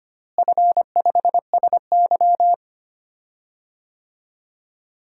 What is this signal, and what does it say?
Morse code, heard as a keyed tone, about 25 wpm, 700 Hz. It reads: F5SY
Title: DXCC-100 Call Signs - 4 Characters - Repeated 3x - 25wpm